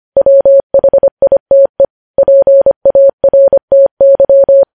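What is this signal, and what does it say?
Morse code, heard as a keyed tone, about 25 wpm, 550 Hz. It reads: WHITE PARTY
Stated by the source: Morse code news headlines sent at 25 wpm.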